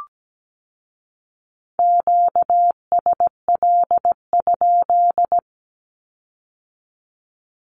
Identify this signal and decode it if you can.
Morse code, heard as a keyed tone, about 17 wpm, 700 Hz. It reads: QSL?
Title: Q-Codes with Spoken Meaning - 17wpm